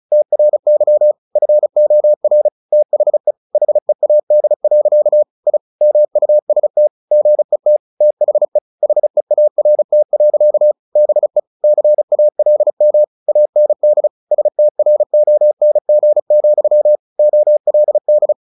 Morse code, heard as a keyed tone, about 34 wpm, 600 Hz. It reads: TRY FOR THE HEAD. I MUST GET THE HEART. BE CALM AND STRONG, OLD
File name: the old man and the sea 2 35wpm